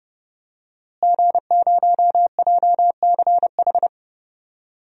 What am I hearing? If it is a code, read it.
Morse code, heard as a keyed tone, about 30 wpm, 700 Hz. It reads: G0JCH
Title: DXCC-100 Call Signs - 5 Characters - Repeated 2x - 30wpm